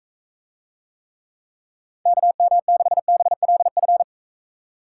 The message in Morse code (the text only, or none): KM6BLF